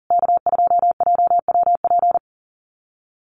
Morse code, heard as a keyed tone, about 40 wpm, 700 Hz. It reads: K2JWP
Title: USA Call Signs Repeated 3x 40wpm